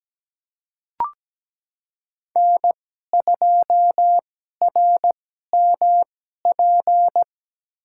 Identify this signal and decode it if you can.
Morse code, heard as a keyed tone, about 17 wpm, spaced as for 12 wpm, 700 Hz. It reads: N2RMP
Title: USA Call Signs - 5 Characters - 2x Character Spacing - 17wpm